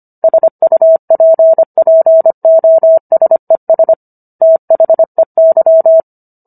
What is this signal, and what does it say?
Morse code, heard as a keyed tone, about 25 wpm, 650 Hz. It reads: SUPPOSES THEY